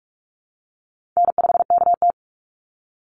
Morse code, heard as a keyed tone, about 44 wpm, 700 Hz. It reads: N5KT
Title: Call Signs Repeated 3x - 45wpm